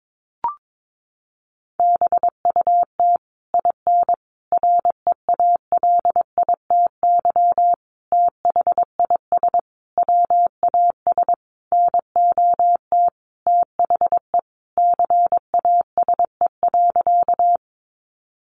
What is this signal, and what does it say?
Morse code, heard as a keyed tone, about 22 wpm, 700 Hz. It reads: BUT IN REALITY THIS WAS NOT THE CASE.
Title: Sentences from Top 1200 Words 22wpm